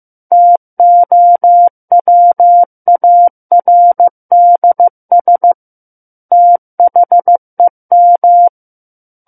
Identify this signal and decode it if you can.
Morse code, heard as a keyed tone, about 15 wpm, 700 Hz. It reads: TOWARDS THEM
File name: Chapter 6 - 15 WPM